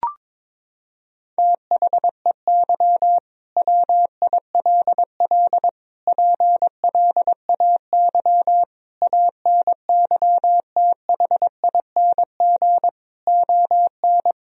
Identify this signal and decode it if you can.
Morse code, heard as a keyed tone, about 22 wpm, 700 Hz. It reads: THEY WILL PLAY ANYTHING ON
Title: Sentences from Top 1000 Words - Review Part 4 - 22wpm